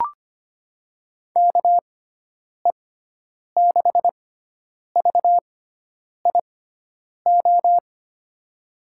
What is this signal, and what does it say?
Morse code, heard as a keyed tone, about 25 wpm, spaced as for 9 wpm, 700 Hz. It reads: KE6VIO
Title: USA Call Signs - 6 Characters - 6x Character Spacing - 25wpm